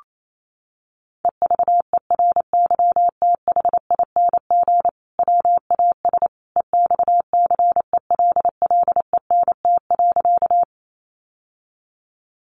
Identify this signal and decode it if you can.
Morse code, heard as a keyed tone, about 28 wpm, 700 Hz. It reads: EVERYTHING WAS EXCELLENT.
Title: Sentences from Top 1000 Words 28wpm